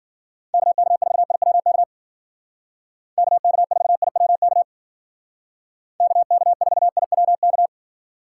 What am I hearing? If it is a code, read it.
Morse code, heard as a keyed tone, about 53 wpm, 700 Hz. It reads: KK4IWK KK4IWK KK4IWK